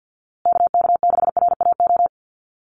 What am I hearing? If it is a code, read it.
Morse code, heard as a keyed tone, about 52 wpm, 700 Hz. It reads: KK6RAO
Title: USA Call Signs Repeated 3x 50wpm